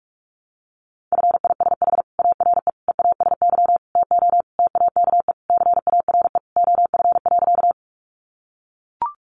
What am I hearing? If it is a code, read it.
Morse code, heard as a keyed tone, about 44 wpm, 700 Hz. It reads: FISH ARE EASY TO TAKE CARE OF.
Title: Easy Phrases 45wpm